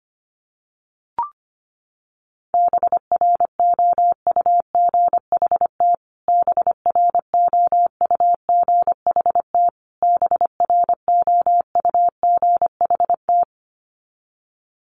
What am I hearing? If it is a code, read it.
Morse code, heard as a keyed tone, about 25 wpm, 700 Hz. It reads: BROUGHT BROUGHT BROUGHT
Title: Top 500 Words Repeated 25wpm